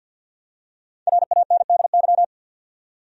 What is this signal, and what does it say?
Morse code, heard as a keyed tone, about 51 wpm, 700 Hz. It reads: RANDY
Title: USA Common Names - 50wpm